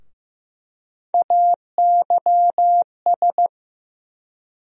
Morse code, heard as a keyed tone, about 16 wpm, 700 Hz. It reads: AYS
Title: USA Call Sign Postfixes - 3 Letters 15wpm